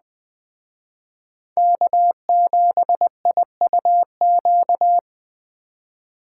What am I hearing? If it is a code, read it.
Morse code, heard as a keyed tone, about 21 wpm, 700 Hz. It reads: K7IUQ